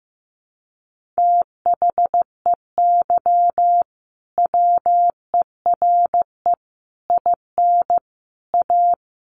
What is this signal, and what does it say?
Morse code, heard as a keyed tone, about 15 wpm, 700 Hz. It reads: THEY WERE IN A
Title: Sentences from Top 1400 Words 15wpm